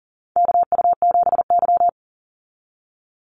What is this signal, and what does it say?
Morse code, heard as a keyed tone, about 40 wpm, 700 Hz. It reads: KU7Y
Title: Call Signs Repeated 2x - 40wpm